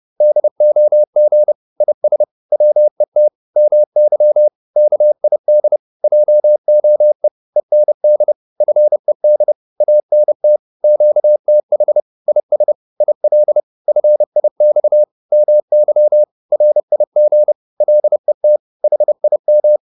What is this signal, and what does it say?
Morse code, heard as a keyed tone, about 30 wpm, 600 Hz. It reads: DOG IS WET MY KID JOE END FED ANT QTH IS IL FIX MY RIG LET HIM